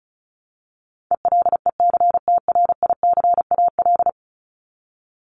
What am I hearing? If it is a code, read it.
Morse code, heard as a keyed tone, about 34 wpm, 700 Hz. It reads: ELECTRICAL
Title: Top 751-1000 Words 35wpm